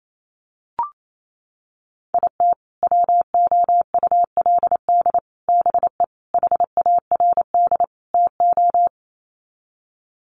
Morse code, heard as a keyed tone, about 28 wpm, 700 Hz. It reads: IT WOULD BE HARD TO
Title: Sets of 5 Words Encore2 28wpm